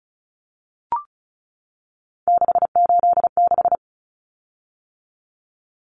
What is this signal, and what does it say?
Morse code, heard as a keyed tone, about 34 wpm, 700 Hz. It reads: 686
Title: Common Cut Numbers - 3-Digit 35wpm